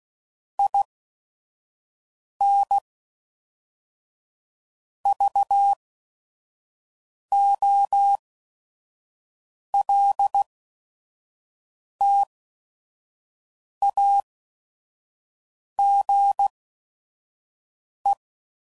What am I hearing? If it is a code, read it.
Morse code, heard as a keyed tone, about 16 wpm, spaced as for 5 wpm, 800 Hz. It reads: IN VOLTAGE
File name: CW Code Groups @5WPM / Technical Subjects / Reactance